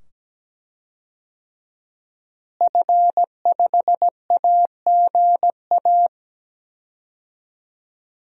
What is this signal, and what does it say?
Morse code, heard as a keyed tone, about 17 wpm, 700 Hz. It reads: F5AGA
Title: DXCC-100 Call Signs - 5 Characters - Repeated 2x - 17wpm